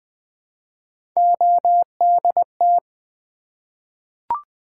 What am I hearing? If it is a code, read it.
Morse code, heard as a keyed tone, about 20 wpm, 700 Hz. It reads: ODT
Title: Three Letter-Number Combinations - 20wpm